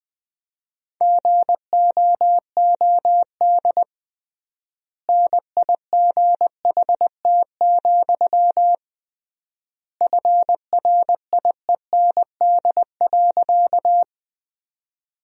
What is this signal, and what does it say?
Morse code, heard as a keyed tone, about 20 wpm, 700 Hz. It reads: GOOD NIGHT, FRIEND.
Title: 3x Word-Spacing - Sentences from Top 300 Words - 20wpm